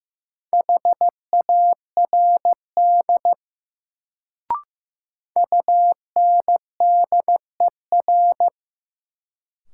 Morse code, heard as a keyed tone, about 15 wpm, 700 Hz. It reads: HARD UNDER
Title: Sentences from Top 300 Words - New Word Drill - 15wpm